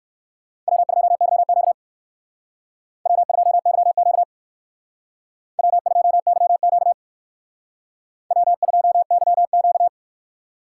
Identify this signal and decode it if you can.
Morse code, heard as a keyed tone, about 57 wpm, 700 Hz. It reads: W2YQ W2YQ W2YQ W2YQ